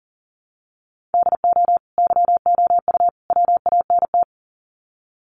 Morse code, heard as a keyed tone, about 38 wpm, 700 Hz. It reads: DO YOU WANT
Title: Sets of 3 Words 40wpm